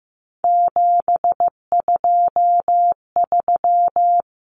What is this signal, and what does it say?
Morse code, heard as a keyed tone, about 15 wpm, 700 Hz. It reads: 723
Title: Numbers - 3-Digit 15wpm